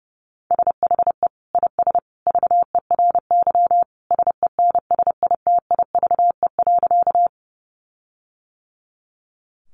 Morse code, heard as a keyed tone, about 28 wpm, 700 Hz. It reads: SHE IS VERY SENSITIVE.
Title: Sentences from Top 1400 Words 30wpm